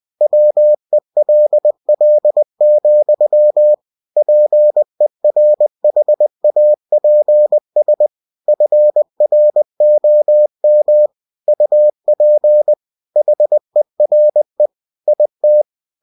Morse code, heard as a keyed tone, about 20 wpm, 600 Hz. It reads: WELL, PERHAPS FROM UP HERE IT
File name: cw-books / winnie-the-pooh / winnie-20wpm-page-0006